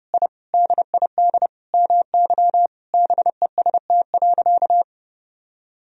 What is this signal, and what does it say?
Morse code, heard as a keyed tone, about 30 wpm, 700 Hz. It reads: I DID MY BEST.
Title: Sentences from Top 1000 Words - Review Part 4 - 30wpm